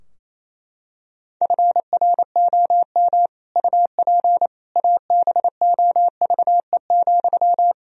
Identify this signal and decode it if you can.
Morse code, heard as a keyed tone, about 28 wpm, 700 Hz. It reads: FROM UP ABOVE,
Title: Sentences from Top 1000 Words - Review Part 2 - 28wpm